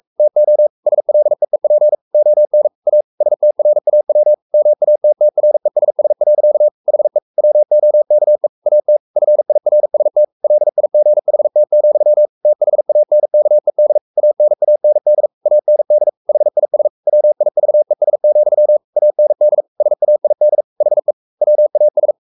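Morse code, heard as a keyed tone, about 44 wpm, 600 Hz. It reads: TO SLEEP ON A STRAW MATTRESS. HE WOKE AT FIRST LIGHT, THANKED ANAND AND HIS WIVES, AND SAID HE WAS
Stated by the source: Morse code at 43 wpm sending Ken Follett's novel Never in English, from CW med Gnister.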